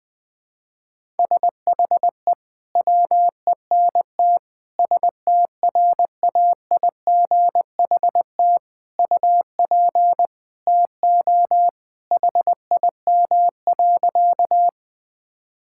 Morse code, heard as a keyed tone, about 19 wpm, 700 Hz. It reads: SHE WENT STRAIGHT UP TO HIM.
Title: Sentences from Top 1000 Words - Review Part 6 - 20wpm